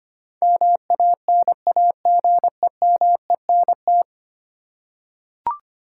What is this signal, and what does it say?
Morse code, heard as a keyed tone, about 25 wpm, 700 Hz. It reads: MANAGEMENT
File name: Ten Letter Words - 25wpm